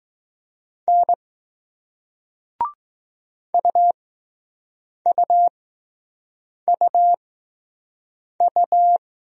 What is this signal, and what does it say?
Morse code, heard as a keyed tone, about 20 wpm, spaced as for 6 wpm, 700 Hz. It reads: N UUUU